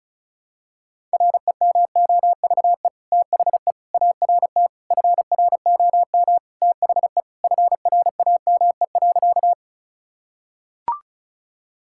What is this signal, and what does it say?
Morse code, heard as a keyed tone, about 34 wpm, 690 Hz. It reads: REMOVE THE ART FROM THE FRAME.
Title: Sentences from Top 1000 Words - Review Part 6 - 35wpm